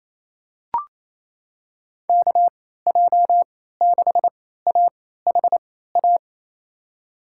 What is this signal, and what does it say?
Morse code, heard as a keyed tone, about 28 wpm, spaced as for 16 wpm, 700 Hz. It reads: KJ6AHA
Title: USA Call Signs - 6 Characters - 3x Character Spacing - 28wpm